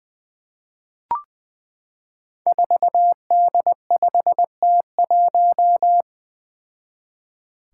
Morse code, heard as a keyed tone, about 20 wpm, 700 Hz. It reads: 4D5T1
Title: Five Challenging Letter-Number Combinations 20wpm